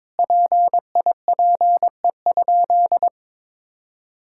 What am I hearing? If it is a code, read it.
Morse code, heard as a keyed tone, about 22 wpm, 700 Hz. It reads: PIPE?
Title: Sentences from Top 1600 Words 22wpm